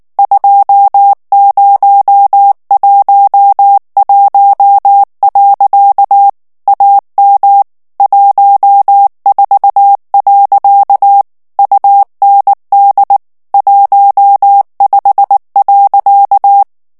Morse code, heard as a keyed tone, about 19 wpm, 800 Hz. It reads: 2011. AM 14. UND 15.
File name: morse mp3 / ALLEMAND texte clair 2 vitesse 18 mots minute